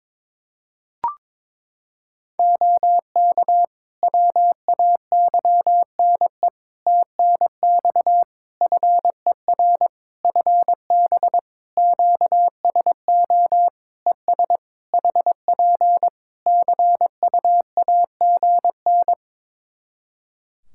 Morse code, heard as a keyed tone, about 22 wpm, 700 Hz. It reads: OK WAYNE TNX FER FB QSO ES HP CUAGN
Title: Practice QSO - Exchange 4 - Ending - 22wpm